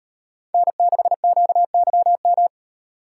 Morse code, heard as a keyed tone, about 35 wpm, 700 Hz. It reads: N6QYM